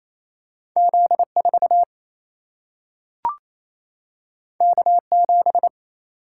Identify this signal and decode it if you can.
Morse code, heard as a keyed tone, about 28 wpm, 700 Hz. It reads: Z 4 K 7
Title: Single-Letter Single-Number 28wpm